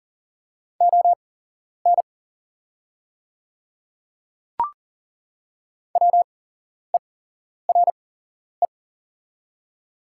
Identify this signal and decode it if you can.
Morse code, heard as a keyed tone, about 40 wpm, spaced as for 11 wpm, 700 Hz. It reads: ON WERE
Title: 8x Character Spacing - Top 100 Words - 40wpm